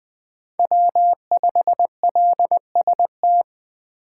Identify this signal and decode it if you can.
Morse code, heard as a keyed tone, about 20 wpm, 700 Hz. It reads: W5LST
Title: 2023 POTA Activator Call Signs - 20wpm